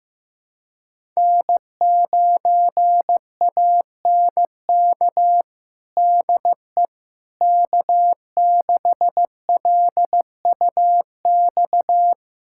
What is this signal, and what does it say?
Morse code, heard as a keyed tone, about 15 wpm, 700 Hz. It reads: N9ANK DE K6LUX
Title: USA Callsign DE Callsign 15wpm